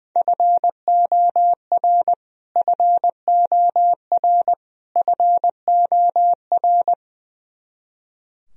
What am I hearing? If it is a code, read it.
Morse code, heard as a keyed tone, about 20 wpm, 700 Hz. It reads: FOR FOR FOR
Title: Top 100 Words Repeated 20wpm